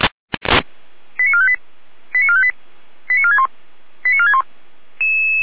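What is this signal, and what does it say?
Five-tone selective call (ZVEI), heard as quick runs of five tones, tones of about 70 ms. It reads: 89458 89458 89461 89461
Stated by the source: Received by SDR, ZVEI five-tone call 89354.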